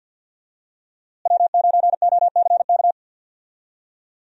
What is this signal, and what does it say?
Morse code, heard as a keyed tone, about 49 wpm, 700 Hz. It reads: W9OCK
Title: USA Call Signs - 5 Characters - Repeated 2x - 50wpm